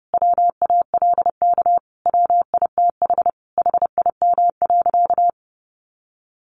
Morse code, heard as a keyed tone, about 30 wpm, 700 Hz. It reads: WALK WITH HIM.
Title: Sentences from Top 1400 Words 30wpm